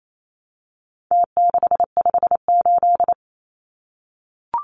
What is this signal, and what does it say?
Morse code, heard as a keyed tone, about 28 wpm, 700 Hz. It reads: T658
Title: Numbers - Contest Style - 4-Digit 28wpm